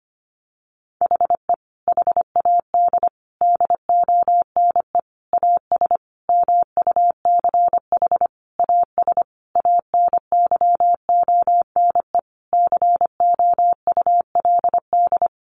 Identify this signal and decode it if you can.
Morse code, heard as a keyed tone, about 25 wpm, 700 Hz. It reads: HE HAD DONE AS MUCH AS ANYONE COULD
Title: Sentences from Top 1200 Words 25wpm